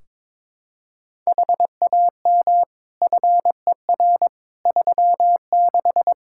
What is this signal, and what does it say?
Morse code, heard as a keyed tone, about 21 wpm, 700 Hz. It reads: HAM FER 36
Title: Practice QSO - Exchange 3 - More Info - 22wpm